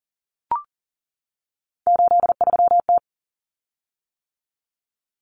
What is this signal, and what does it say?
Morse code, heard as a keyed tone, about 40 wpm, 700 Hz. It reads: 83T
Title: Common Cut Numbers - 3-Digit 40wpm